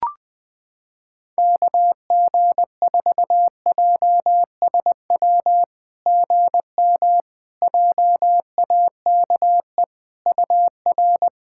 Morse code, heard as a keyed tone, about 20 wpm, 700 Hz. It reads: KG4JSW GM JAKE UR
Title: SOTA QSO Practice 20wpm